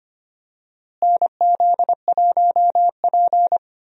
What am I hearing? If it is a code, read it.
Morse code, heard as a keyed tone, about 25 wpm, 700 Hz. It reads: NZ1P
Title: USA Call Signs - 4 Characters - Repeated 3x - 25wpm